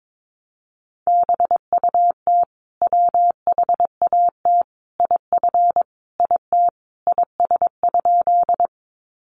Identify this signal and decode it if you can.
Morse code, heard as a keyed tone, about 22 wpm, 700 Hz. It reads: BUT WHAT IF IT IS?
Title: Sentences from Top 100 Words 22wpm